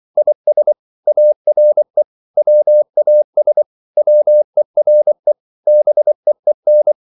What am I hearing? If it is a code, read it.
Morse code, heard as a keyed tone, about 24 wpm, 600 Hz. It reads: IS ARE WAS WERE BEEN
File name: CW / Practice / 500MostCommonEnglishWords24